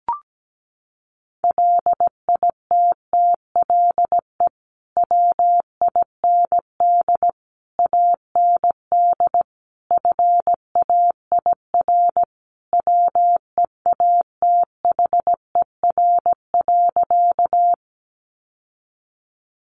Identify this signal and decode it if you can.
Morse code, heard as a keyed tone, about 17 wpm, 700 Hz. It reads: LITTLE WIND AND FAIR WEATHER.
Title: Sentences from Top 1300 Words 17wpm